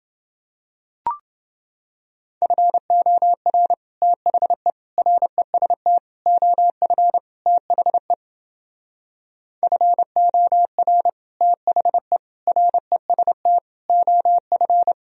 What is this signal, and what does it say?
Morse code, heard as a keyed tone, about 30 wpm, 700 Hz. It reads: FOR THE REST OF THE FOR THE REST OF